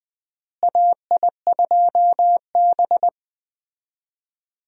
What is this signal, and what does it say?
Morse code, heard as a keyed tone, about 21 wpm, 700 Hz. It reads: AI2B
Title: USA Call Signs - 4 Characters - Repeated 2x - 20wpm